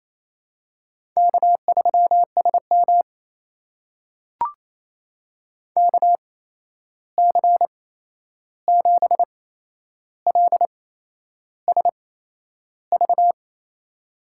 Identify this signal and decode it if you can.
Morse code, heard as a keyed tone, about 27 wpm, spaced as for 10 wpm, 700 Hz. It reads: K3SM KC7LSV
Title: USA Call Signs - 8x Character Spacing - 28wpm